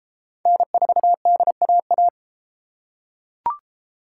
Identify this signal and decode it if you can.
Morse code, heard as a keyed tone, about 30 wpm, 700 Hz. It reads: N4DAA